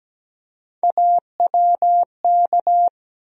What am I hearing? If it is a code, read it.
Morse code, heard as a keyed tone, about 17 wpm, 700 Hz. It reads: AWK
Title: USA Call Sign Postfixes - 3 Letters 17wpm